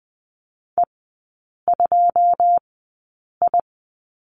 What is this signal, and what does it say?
Morse code, heard as a keyed tone, about 20 wpm, spaced as for 8 wpm, 690 Hz. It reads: E2I
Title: 212 Lesson 34e - Number 8 - Three Characters 20wpm